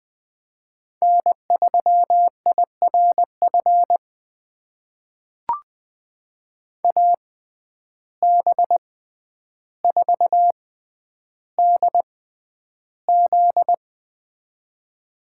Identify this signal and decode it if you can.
Morse code, heard as a keyed tone, about 20 wpm, spaced as for 10 wpm, 700 Hz. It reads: N3IRF AB4DZ
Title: USA Call Signs - 5 Characters - 6x Character Spacing - 20wpm